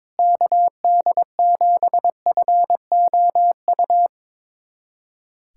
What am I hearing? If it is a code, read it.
Morse code, heard as a keyed tone, about 22 wpm, 700 Hz. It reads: KD7FOU
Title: USA Callsign DE Callsign 22wpm